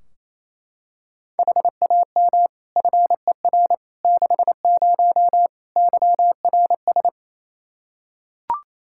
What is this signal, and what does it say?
Morse code, heard as a keyed tone, about 27 wpm, 700 Hz. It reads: HAM FER 60 YRS
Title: K1USN SST QSO Practice 28wpm